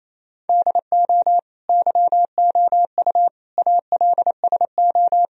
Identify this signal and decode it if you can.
Morse code, heard as a keyed tone, about 28 wpm, 700 Hz. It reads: DO YOU ALSO
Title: Sentences from Top 1000 Words - Review Part 1 - 28wpm